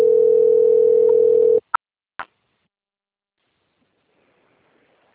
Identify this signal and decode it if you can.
Ringback tone (North America), 440 + 480 Hz.
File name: rg-1600-unknown- - - .73623